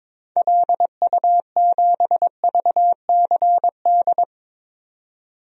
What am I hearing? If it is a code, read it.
Morse code, heard as a keyed tone, about 22 wpm, 700 Hz. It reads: LU7VCD
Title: DXCC-100 Call Signs - 6 Characters - 22wpm